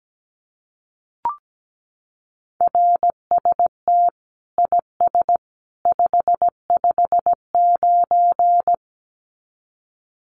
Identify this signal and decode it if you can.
Morse code, heard as a keyed tone, about 17 wpm, 700 Hz. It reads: RST IS 559
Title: Common QSO Phrases 17wpm